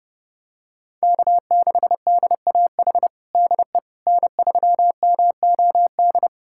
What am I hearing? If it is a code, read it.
Morse code, heard as a keyed tone, about 30 wpm, 700 Hz. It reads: K6DAH DE N3MOD
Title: USA Callsign DE Callsign - 5 Characters - 30wpm